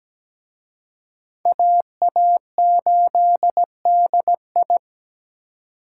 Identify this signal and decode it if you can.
Morse code, heard as a keyed tone, about 20 wpm, 700 Hz. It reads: AA8DI
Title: Speed Racing - USA 2x2 Call Signs 17wpm